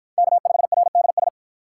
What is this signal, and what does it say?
Morse code, heard as a keyed tone, about 53 wpm, 700 Hz. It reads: K5RDS